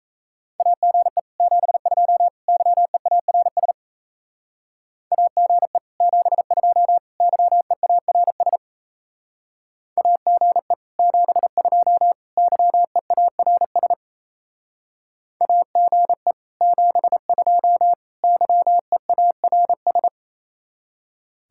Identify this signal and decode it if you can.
Morse code, heard as a keyed tone, about 37 wpm, 700 Hz. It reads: AGE 72 YEARS AGE 72 YEARS AGE 72 YEARS AGE 72 YEARS